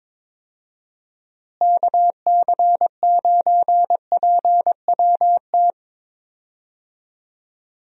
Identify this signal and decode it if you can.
Morse code, heard as a keyed tone, about 22 wpm, 700 Hz. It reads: KC9PWT
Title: USA Call Signs - 6 Characters - Repeated 2x - 22wpm